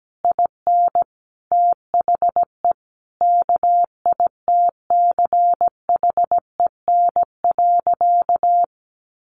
Morse code, heard as a keyed tone, about 17 wpm, 700 Hz. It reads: IN THE KITCHEN.
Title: Sentences from Top 1500 Words 17wpm